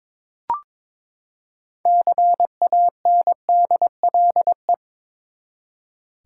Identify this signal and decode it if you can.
Morse code, heard as a keyed tone, about 22 wpm, 700 Hz. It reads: CANDLE
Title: Six Letter Words - 22wpm